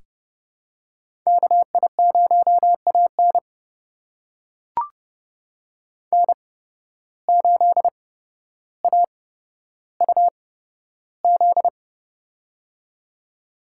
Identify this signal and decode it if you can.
Morse code, heard as a keyed tone, about 30 wpm, spaced as for 13 wpm, 700 Hz. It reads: KI0AN N8AUZ
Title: USA Call Signs - 8x Character Spacing - 30wpm